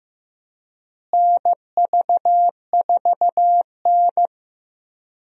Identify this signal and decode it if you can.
Morse code, heard as a keyed tone, about 15 wpm, 700 Hz. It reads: NV4N
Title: USA Call Signs - 4 Characters - Repeated 2x - 15wpm